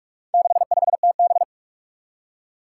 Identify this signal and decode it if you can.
Morse code, heard as a keyed tone, about 44 wpm, 700 Hz. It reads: 65TB
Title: Four Challenging Letter-Number Combinations 45wpm